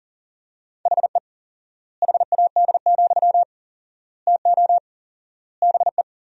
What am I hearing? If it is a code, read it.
Morse code, heard as a keyed tone, about 37 wpm, 700 Hz. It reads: HE HAD, TO BE
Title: 4x Word-Spacing - Sentences from Top 500 Words - 40wpm